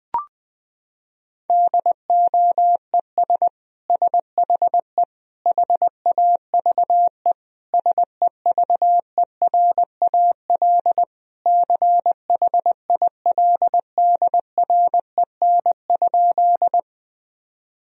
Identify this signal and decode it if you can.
Morse code, heard as a keyed tone, about 21 wpm, 700 Hz. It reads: DOES SHE HAVE SEVERAL CHILDREN?
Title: Sentences from Top 1000 Words - Review Part 2 - 20wpm